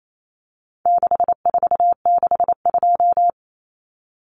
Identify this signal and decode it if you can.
Morse code, heard as a keyed tone, about 28 wpm, 700 Hz. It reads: 6462
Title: Numbers - 4-Digit 28wpm